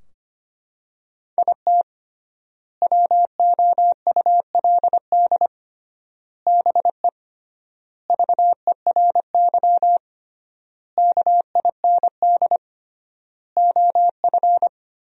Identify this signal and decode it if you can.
Morse code, heard as a keyed tone, about 25 wpm, 700 Hz. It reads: IT WOULD BE VERY KIND OF
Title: 3x Word-Spacing - Sentences from Top 200 Words - 25wpm